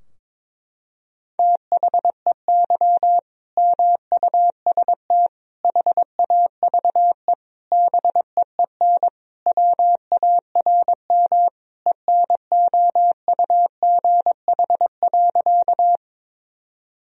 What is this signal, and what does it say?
Morse code, heard as a keyed tone, about 22 wpm, 700 Hz. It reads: THEY MUST HAVE BEEN WARM ENOUGH.
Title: Sentences from Top 1000 Words - Review Part 2 - 22wpm